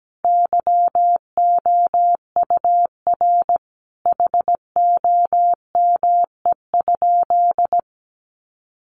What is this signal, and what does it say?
Morse code, heard as a keyed tone, about 17 wpm, 700 Hz. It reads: YOUR HOME?
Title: Sentences from Top 1000 Words 17wpm